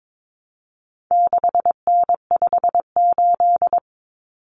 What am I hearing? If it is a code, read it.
Morse code, heard as a keyed tone, about 22 wpm, 700 Hz. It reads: 6N58
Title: Common Cut Numbers - 4-Digit 22wpm